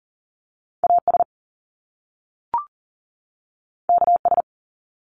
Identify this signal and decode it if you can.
Morse code, heard as a keyed tone, about 42 wpm, 700 Hz. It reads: AS KS
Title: Top Two Letter Combinations in Words 40wpm